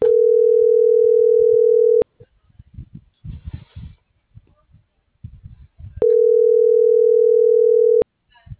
Ringback tone (North America), 440 + 480 Hz.